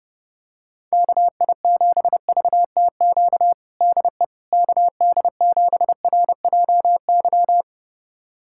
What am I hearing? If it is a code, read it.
Morse code, heard as a keyed tone, about 30 wpm, 700 Hz. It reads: KI7VTQ DE KD7RJY
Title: USA Callsign DE Callsign - 6 Characters - 30wpm